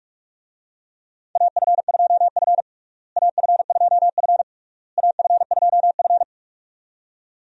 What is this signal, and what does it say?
Morse code, heard as a keyed tone, about 46 wpm, 700 Hz. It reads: AF2F AF2F AF2F
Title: USA Call Signs - 4 Characters - Repeated 3x - 45wpm